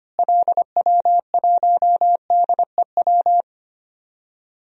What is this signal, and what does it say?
Morse code, heard as a keyed tone, about 25 wpm, 700 Hz. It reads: LW1DEW
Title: DXCC-100 Call Signs - 6 Characters - Repeated 2x - 25wpm